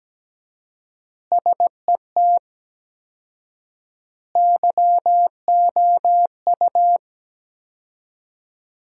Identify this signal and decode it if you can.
Morse code, heard as a keyed tone, about 17 wpm, 700 Hz. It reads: SET YOU
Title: 4x Word-Spacing - Sentences from Top 200 Words - 17wpm